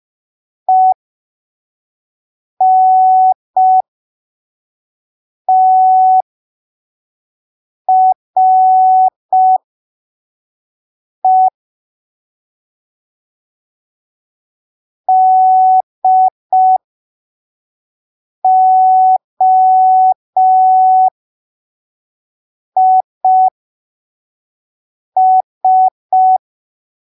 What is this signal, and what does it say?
Morse code, heard as a keyed tone, about 5 wpm, spaced as for 3 wpm, 750 Hz. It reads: ENTRE DOIS
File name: arquivos / texto / texto 04 5ppm